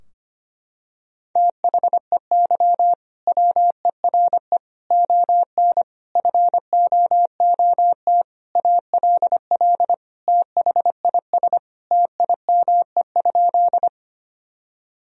Morse code, heard as a keyed tone, about 25 wpm, 700 Hz. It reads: THEY WERE ON FOOT ALL THIS TIME?
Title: Sentences from Top 1000 Words - Review Part 3 - 25wpm